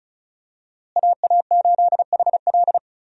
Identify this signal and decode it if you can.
Morse code, heard as a keyed tone, about 36 wpm, 700 Hz. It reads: AA8HL
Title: USA Call Signs - 5 Characters - Repeated 3x - 35wpm